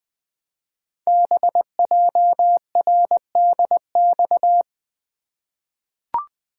Morse code, heard as a keyed tone, about 20 wpm, 700 Hz. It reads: BJRDX
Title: Code Groups - 5 Letters - 20wpm